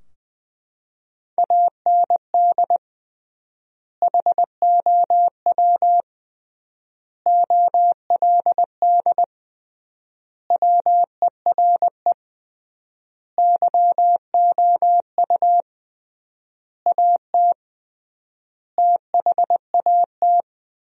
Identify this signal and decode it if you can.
Morse code, heard as a keyed tone, about 20 wpm, 700 Hz. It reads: AND HOW OLD WERE YOU AT THAT
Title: 3x Word-Spacing - Sentences from Top 200 Words - 20wpm